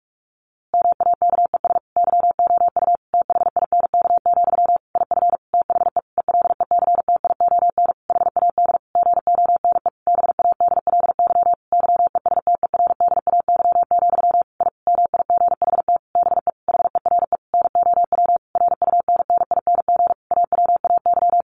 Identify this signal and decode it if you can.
Morse code, heard as a keyed tone, about 44 wpm, 700 Hz. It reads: MAKES YOU THINK, IF THE ELECTION HAD GONE BADLY YESTERDAY, I MIGHT BE HERE NOW RUNNING AWAY
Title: A Sound Of Thunder 45wpm